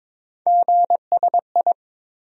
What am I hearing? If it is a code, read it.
Morse code, heard as a keyed tone, about 22 wpm, 700 Hz. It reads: GSI
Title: USA Call Sign Postfixes - 3 Letters 22wpm